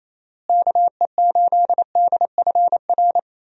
Morse code, heard as a keyed tone, about 28 wpm, 700 Hz. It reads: KE8DFR
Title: USA Call Signs - 6 Characters - Repeated 3x - 28wpm